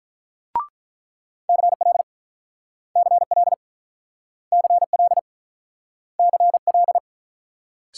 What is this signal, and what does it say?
Morse code, heard as a keyed tone, about 49 wpm, 700 Hz. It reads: CL CL CL CL